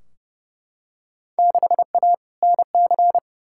Morse code, heard as a keyed tone, about 30 wpm, 700 Hz. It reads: 6A NC
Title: Field Day Exchanges Encore - 30wpm